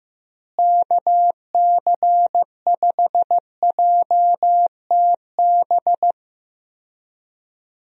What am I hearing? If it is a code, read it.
Morse code, heard as a keyed tone, about 15 wpm, 700 Hz. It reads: KC5JTB
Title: USA Call Signs - 6 Characters - Repeated 3x - 15wpm